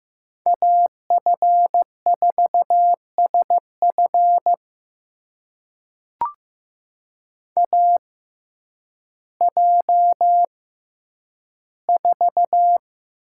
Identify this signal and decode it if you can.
Morse code, heard as a keyed tone, about 15 wpm, spaced as for 9 wpm, 700 Hz. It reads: AF4SF AJ4
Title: USA Call Signs - 5 Characters - 6x Character Spacing - 15wpm